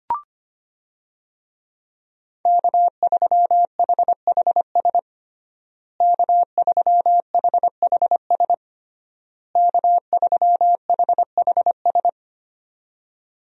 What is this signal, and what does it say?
Morse code, heard as a keyed tone, about 25 wpm, 700 Hz. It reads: K3HHS K3HHS K3HHS
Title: USA Call Signs - 5 Characters - Repeated 3x - 25wpm